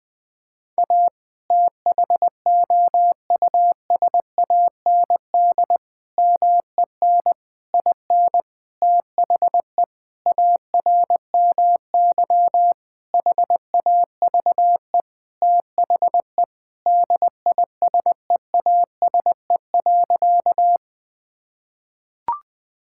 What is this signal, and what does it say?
Morse code, heard as a keyed tone, about 20 wpm, 700 Hz. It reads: A THOUSAND MEN IN THE ARMY HAVE THE DISEASE.
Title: Sentences from Top 1000 Words - Review Part 6 - 20wpm